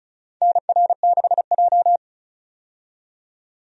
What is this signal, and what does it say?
Morse code, heard as a keyed tone, about 36 wpm, 700 Hz. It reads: NR6J
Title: USA Call Signs - 4 Characters - 35wpm